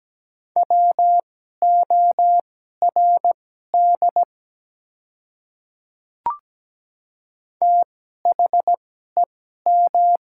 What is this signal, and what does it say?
Morse code, heard as a keyed tone, about 17 wpm, spaced as for 12 wpm, 700 Hz. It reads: WORD THEM
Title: 2x Character Spacing - Top 100 Words - 17wpm